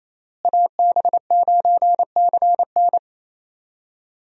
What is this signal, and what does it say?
Morse code, heard as a keyed tone, about 28 wpm, 700 Hz. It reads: AB9CN
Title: USA Call Signs - 5 Characters - Repeated 3x - 28wpm